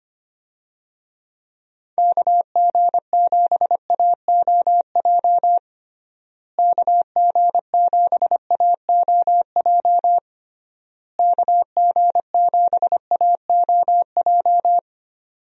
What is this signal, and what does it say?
Morse code, heard as a keyed tone, about 25 wpm, 700 Hz. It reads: KG7AOJ KG7AOJ KG7AOJ
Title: USA Call Signs - 6 Characters - Repeated 3x - 25wpm